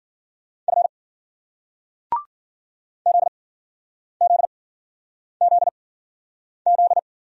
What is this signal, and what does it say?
Morse code, heard as a keyed tone, about 50 wpm, spaced as for 9 wpm, 700 Hz. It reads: F ZZZZ